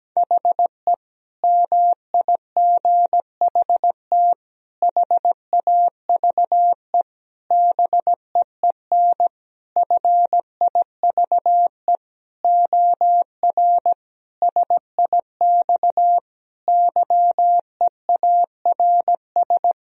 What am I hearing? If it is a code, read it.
Morse code, heard as a keyed tone, about 17 wpm, 700 Hz. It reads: HE MIGHT HAVE BEEN FIVE OR SIX YEARS
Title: Sentences from Top 1000 Words - Review Part 2 - 17wpm